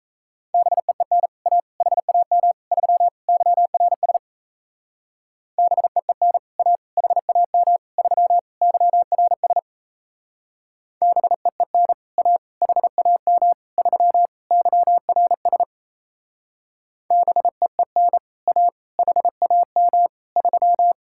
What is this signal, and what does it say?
Morse code, heard as a keyed tone, about 37 wpm, 700 Hz. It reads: BEEN A HAM 3 YRS BEEN A HAM 3 YRS BEEN A HAM 3 YRS BEEN A HAM 3